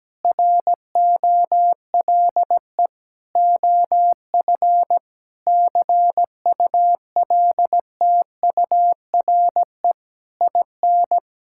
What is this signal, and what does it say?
Morse code, heard as a keyed tone, about 17 wpm, 700 Hz. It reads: ROLE OF CULTURE IN
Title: Sentences from Top 1000 Words - Review Part 6 - 17wpm